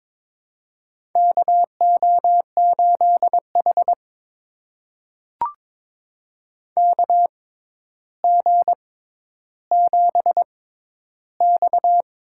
Morse code, heard as a keyed tone, about 22 wpm, spaced as for 11 wpm, 700 Hz. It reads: KO8H KG7X
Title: USA Call Signs - 4 Characters - 6x Character Spacing - 22wpm